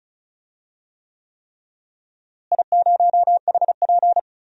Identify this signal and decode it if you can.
Morse code, heard as a keyed tone, about 34 wpm, 700 Hz. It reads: I0HP
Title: DXCC-100 Call Signs - 4 Characters - Repeated 3x - 35wpm